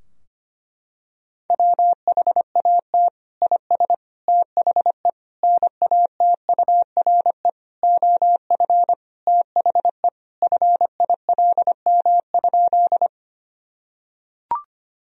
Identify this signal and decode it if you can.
Morse code, heard as a keyed tone, about 25 wpm, 700 Hz. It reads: WHAT IS THE NATURE OF THE FILM?
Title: Sentences from Top 1000 Words - Review Part 3 - 25wpm